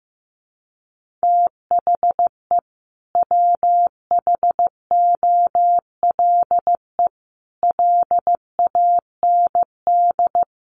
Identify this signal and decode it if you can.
Morse code, heard as a keyed tone, about 15 wpm, 700 Hz. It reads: THE WHOLE LAND
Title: Sentences from Top 1400 Words 15wpm